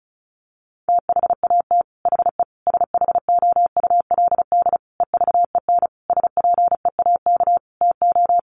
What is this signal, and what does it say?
Morse code, heard as a keyed tone, about 34 wpm, 700 Hz. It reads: THAT HE SHOULD EVEN SPEAK TO
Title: Sentences from Top 1500 Words 35wpm